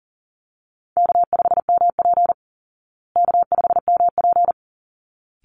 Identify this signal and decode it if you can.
Morse code, heard as a keyed tone, about 40 wpm, 700 Hz. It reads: K5MP K5MP
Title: Easy Call Signs Repeated 2x - 40wpm